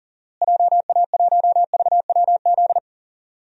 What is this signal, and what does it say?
Morse code, heard as a keyed tone, about 40 wpm, 700 Hz. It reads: JA1VWZ
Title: DXCC-100 Call Signs - 6 Characters - Repeated 3x - 40wpm